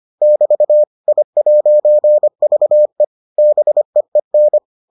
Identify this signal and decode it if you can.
Morse code, heard as a keyed tone, about 25 wpm, 600 Hz. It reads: <BT> I'VE BEEN